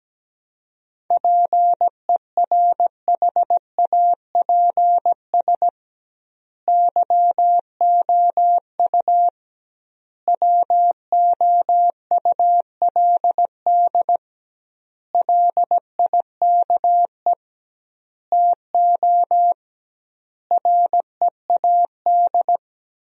Morse code, heard as a keyed tone, about 17 wpm, 700 Hz. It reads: PERHAPS YOU WOULD LIKE TO READ
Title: 2x Word-Spacing - Sentences from Top 500 Words - 17wpm